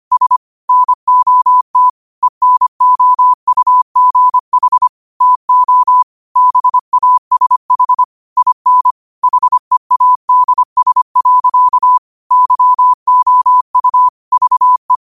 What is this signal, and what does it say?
Morse code, heard as a keyed tone, about 25 wpm, 1000 Hz. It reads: S NOT ENOUGH TO BASH IN HEADS. YOU VE